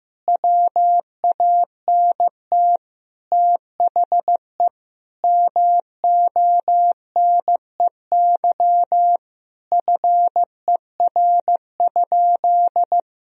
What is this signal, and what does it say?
Morse code, heard as a keyed tone, about 15 wpm, 700 Hz. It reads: WANT THE MONEY FER?
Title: Sentences from Top 400 Words and Abbreviations 15wpm